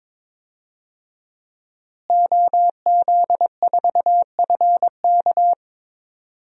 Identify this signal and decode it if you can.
Morse code, heard as a keyed tone, about 22 wpm, 700 Hz. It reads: OZ4FK
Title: DXCC-100 Call Signs - 5 Characters - Repeated 2x - 22wpm